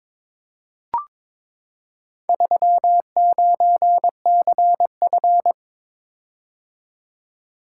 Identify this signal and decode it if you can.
Morse code, heard as a keyed tone, about 22 wpm, 700 Hz. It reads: 39CF
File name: practice / 22wpm / Four Letter-Number Combinations - 22wpm